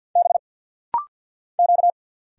Morse code, heard as a keyed tone, about 50 wpm, 700 Hz. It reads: B Q
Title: Single Letter-Number - Rapid-Fire 50wpm